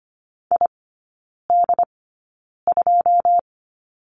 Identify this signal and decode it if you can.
Morse code, heard as a keyed tone, about 25 wpm, spaced as for 9 wpm, 690 Hz. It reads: ID2
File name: Lesson 14e - Number 2 - Three Characters 25wpm